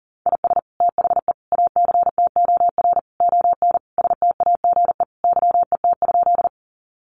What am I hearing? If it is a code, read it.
Morse code, heard as a keyed tone, about 40 wpm, 700 Hz. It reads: IS THE ACTOR ON STAGE YET?
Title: Sentences from Top 1500 Words 40wpm